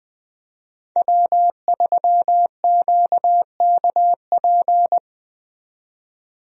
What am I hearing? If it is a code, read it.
Morse code, heard as a keyed tone, about 20 wpm, 700 Hz. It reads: W3QKP
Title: USA Call Signs - 5 Characters - 20wpm